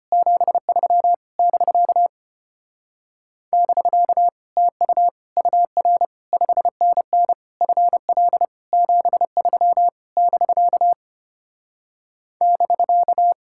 Morse code, heard as a keyed tone, about 33 wpm, 700 Hz. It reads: 73 <BK> <BK> TU UR 5NN FL 73 <BK> <BK>